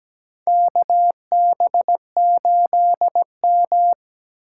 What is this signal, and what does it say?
Morse code, heard as a keyed tone, about 17 wpm, 700 Hz. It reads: KB8M
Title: K1USN SST QSO Practice - 17wpm